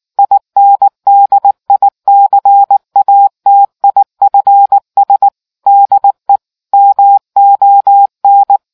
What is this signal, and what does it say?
Morse code, heard as a keyed tone, about 19 wpm, 800 Hz. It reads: INDICATIFS DE MON